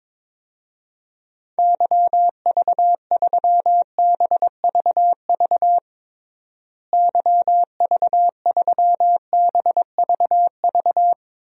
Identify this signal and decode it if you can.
Morse code, heard as a keyed tone, about 22 wpm, 700 Hz. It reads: YV3BVV YV3BVV
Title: DXCC-100 Call Signs - 6 Characters - Repeated 3x - 22wpm